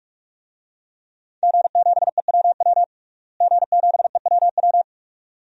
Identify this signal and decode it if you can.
Morse code, heard as a keyed tone, about 46 wpm, 700 Hz. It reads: G7EWW G7EWW
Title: DXCC-100 Call Signs - 5 Characters - Repeated 2x - 45wpm